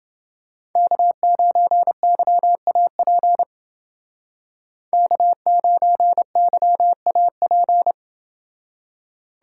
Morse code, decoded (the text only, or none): K9YAP K9YAP